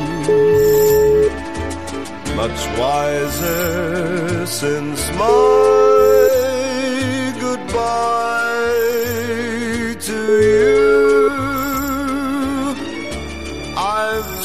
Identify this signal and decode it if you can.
Ringback tone (Europe, 425 Hz), with noise.